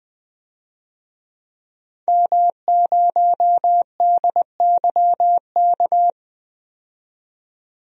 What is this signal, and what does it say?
Morse code, heard as a keyed tone, about 21 wpm, 700 Hz. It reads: M0DYK
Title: DXCC-100 Call Signs - 5 Characters - 20wpm